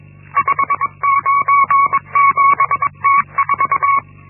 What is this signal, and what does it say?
Morse code, heard as a keyed tone, about 22 wpm, 1120 Hz. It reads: 597T4
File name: Linksmakalnis-1